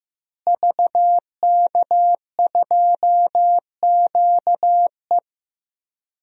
Morse code, heard as a keyed tone, about 16 wpm, 700 Hz. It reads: VK2QE
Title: DXCC-100 Call Signs - 5 Characters - Repeated 3x - 15wpm